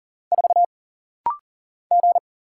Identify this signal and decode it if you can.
Morse code, heard as a keyed tone, about 40 wpm, 700 Hz. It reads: 4 G
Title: Single Letter-Number - Rapid-Fire 40wpm